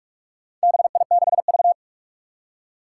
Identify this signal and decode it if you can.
Morse code, heard as a keyed tone, about 44 wpm, 700 Hz. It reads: BI6V
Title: Four Challenging Letter-Number Combinations 45wpm